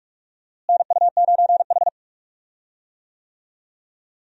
Morse code, heard as a keyed tone, about 44 wpm, 700 Hz. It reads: NU9H
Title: USA Call Signs - 4 Characters - Repeated 3x - 45wpm